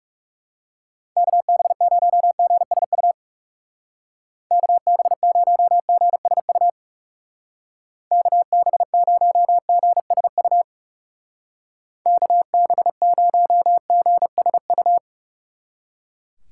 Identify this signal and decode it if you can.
Morse code, heard as a keyed tone, about 37 wpm, 700 Hz. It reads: KB0GSU KB0GSU KB0GSU KB0GSU
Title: Speed Racing - USA 2x3 Call Signs 30wpm